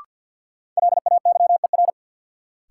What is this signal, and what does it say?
Morse code, heard as a keyed tone, about 52 wpm, 700 Hz. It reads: LAYER
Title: Sentences from Top 1600 Words - New Word Drill Alt - 50wpm